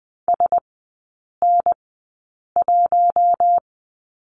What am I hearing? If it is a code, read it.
Morse code, heard as a keyed tone, about 20 wpm, spaced as for 8 wpm, 700 Hz. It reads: SN1
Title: 041 Lesson 8e - Number 1 - Three Characters 20wpm